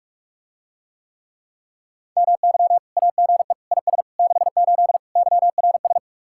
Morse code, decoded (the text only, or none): MY AGE IS 68 YRS